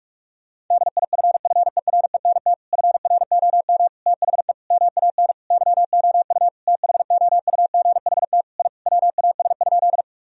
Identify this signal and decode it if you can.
Morse code, heard as a keyed tone, about 43 wpm, 700 Hz. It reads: DIFFERENT FROM THE MAN YOU THOUGHT I WAS?
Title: Sentences from Top 1000 Words - Review Part 3 - 45wpm